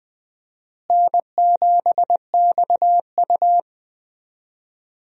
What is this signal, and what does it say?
Morse code, heard as a keyed tone, about 20 wpm, 700 Hz. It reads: N7XU
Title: USA Call Signs - 4 Characters - Repeated 3x - 20wpm